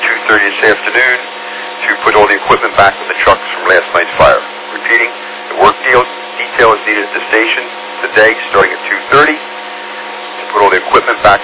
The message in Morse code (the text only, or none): none